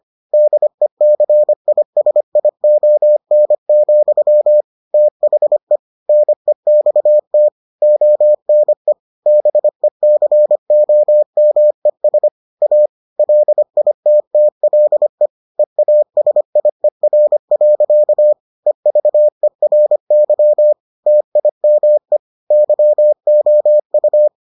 Morse code, heard as a keyed tone, about 25 wpm, 600 Hz. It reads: DECISION, THE NEXT ONE BECOMES A LITTLE EASIER. EVERY TIME YOU